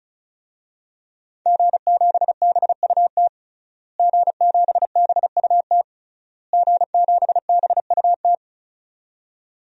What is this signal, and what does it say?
Morse code, heard as a keyed tone, about 34 wpm, 700 Hz. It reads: G7BUT G7BUT G7BUT
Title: DXCC-100 Call Signs - 5 Characters - Repeated 3x - 35wpm